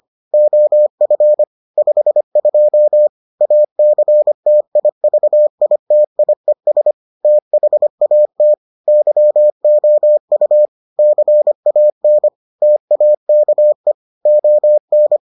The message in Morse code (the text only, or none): OF 52 ACTIVITIES THAT YOU CAN TAKE ON